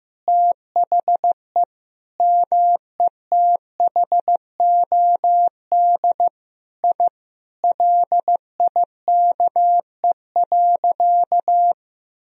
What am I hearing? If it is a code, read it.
Morse code, heard as a keyed tone, about 16 wpm, 700 Hz. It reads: THE METHOD I LIKE.
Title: Sentences from Top 1000 Words - Review Part 3 - 15wpm